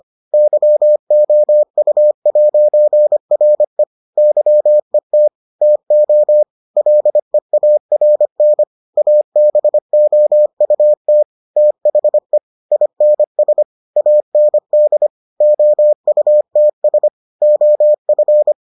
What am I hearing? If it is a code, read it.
Morse code, heard as a keyed tone, about 25 wpm, 600 Hz. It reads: YOU'RE YET TO LEARN ABOUT THE INS AND OUTS OF